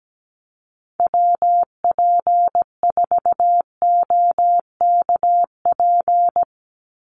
Morse code, heard as a keyed tone, about 17 wpm, 700 Hz. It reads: WP4OKP
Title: USA Call Signs 17wpm